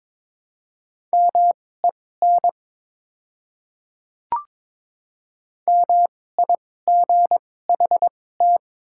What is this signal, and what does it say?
Morse code, heard as a keyed tone, about 21 wpm, spaced as for 16 wpm, 700 Hz. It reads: MEN MIGHT
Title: 2x Character Spacing - Top 300 Words - 22wpm